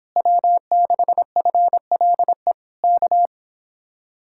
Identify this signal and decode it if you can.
Morse code, heard as a keyed tone, about 26 wpm, 700 Hz. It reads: W6FLE K